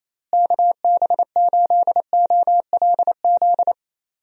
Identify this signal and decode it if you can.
Morse code, heard as a keyed tone, about 28 wpm, 700 Hz. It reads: KB8OLZ
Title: USA Call Signs - 6 Characters - Repeated 3x - 28wpm